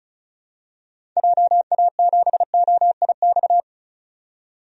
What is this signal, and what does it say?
Morse code, heard as a keyed tone, about 36 wpm, 700 Hz. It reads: JA7OIX
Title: DXCC-100 Call Signs - 6 Characters - 35wpm